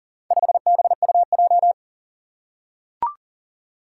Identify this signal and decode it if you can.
Morse code, heard as a keyed tone, about 40 wpm, 700 Hz. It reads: 5BUJ
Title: Four Challenging Letter-Number Combinations 40wpm